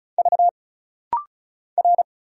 Morse code, heard as a keyed tone, about 34 wpm, 700 Hz. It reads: V R
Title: Single Letter-Number - Rapid-Fire 35wpm